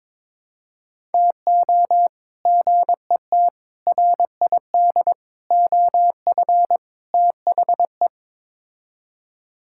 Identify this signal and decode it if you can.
Morse code, heard as a keyed tone, about 22 wpm, 700 Hz. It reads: TO GET RID OF THE
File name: practice / 22wpm / Build A Phrase - 5-Word Phrases - 22wpm